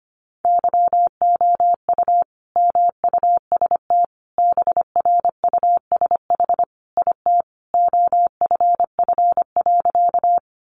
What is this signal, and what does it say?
Morse code, heard as a keyed tone, about 25 wpm, 700 Hz. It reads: YOU MUST BRUSH IT OFF.
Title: Sentences from Top 1100 Words 25wpm